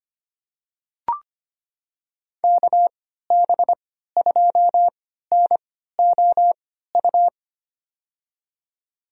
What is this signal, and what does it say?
Morse code, heard as a keyed tone, about 25 wpm, spaced as for 14 wpm, 700 Hz. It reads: KB2NOU
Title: USA Call Signs - 3x Character Spacing - 25wpm